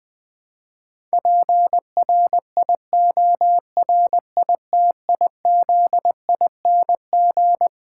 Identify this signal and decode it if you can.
Morse code, heard as a keyed tone, about 21 wpm, 700 Hz. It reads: PRIORITIZING
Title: Difficult Sentences with Hints - 20wpm